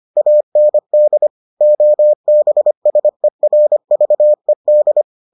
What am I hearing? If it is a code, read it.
Morse code, heard as a keyed tone, about 25 wpm, 600 Hz. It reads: AND OBSERVED